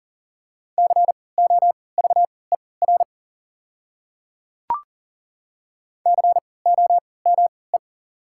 Morse code, heard as a keyed tone, about 40 wpm, spaced as for 23 wpm, 700 Hz. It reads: COVER COME
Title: 3x Character Spacing - Top 300 Words - 40wpm